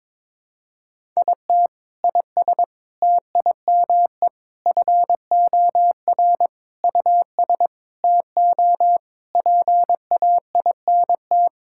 Morse code, heard as a keyed tone, about 22 wpm, 700 Hz. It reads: IT IS TIME FOR US TO PAINT
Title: Sentences from Top 1000 Words - Review Part 6 - 22wpm